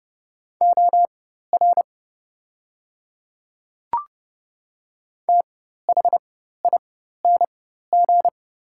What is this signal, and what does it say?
Morse code, heard as a keyed tone, about 30 wpm, spaced as for 14 wpm, 690 Hz. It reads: OR THING
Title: 4x Character Spacing - Top 100 Words - 30wpm